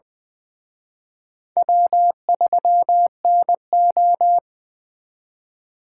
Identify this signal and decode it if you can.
Morse code, heard as a keyed tone, about 24 wpm, 700 Hz. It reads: W3NO
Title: Speed Racing - USA 1x2 Call Signs 20wpm